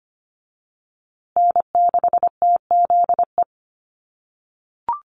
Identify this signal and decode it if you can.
Morse code, heard as a keyed tone, about 25 wpm, 700 Hz. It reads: N6TZE
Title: USA Call Signs Repeated 2x 25wpm